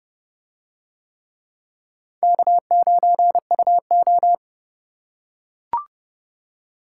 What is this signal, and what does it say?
Morse code, heard as a keyed tone, about 29 wpm, 700 Hz. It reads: K9UO